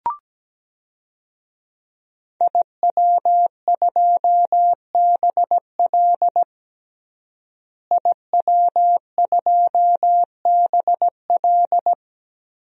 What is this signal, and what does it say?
Morse code, heard as a keyed tone, about 17 wpm, 700 Hz. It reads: IW2BL IW2BL
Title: DXCC-100 Call Signs - 5 Characters - Repeated 3x - 17wpm